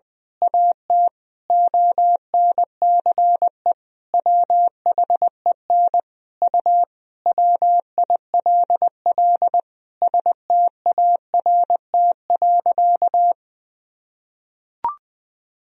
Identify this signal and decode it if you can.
Morse code, heard as a keyed tone, about 20 wpm, 700 Hz. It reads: AT ONCE WHEN U WILL START.
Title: Sentences from Top 300 Words and Abbreviations 20wpm